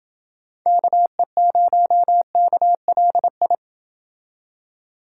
Morse code, heard as a keyed tone, about 28 wpm, 700 Hz. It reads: KE0KLI